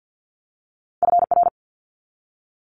Morse code, heard as a keyed tone, about 50 wpm, 700 Hz. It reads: FR
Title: Multi-letter Phonemes 50wpm